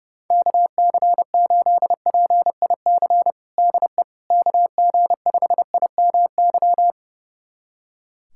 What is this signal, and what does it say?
Morse code, heard as a keyed tone, about 30 wpm, 700 Hz. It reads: KC8PIC DE KG5IMY
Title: USA Callsign DE Callsign - 6 Characters - 30wpm